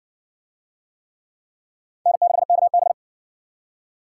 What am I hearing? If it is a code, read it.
Morse code, heard as a keyed tone, about 58 wpm, 700 Hz. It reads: N6KB